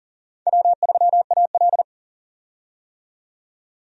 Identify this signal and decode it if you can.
Morse code, heard as a keyed tone, about 37 wpm, 700 Hz. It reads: W3AL
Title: USA Call Signs - 4 Characters - 40wpm